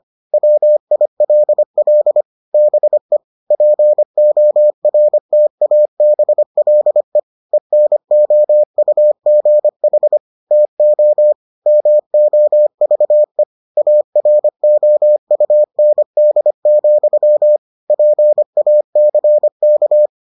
Morse code, heard as a keyed tone, about 25 wpm, 600 Hz. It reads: WILL BE PORTABLE ENOUGH TO MOVE AROUND, PACK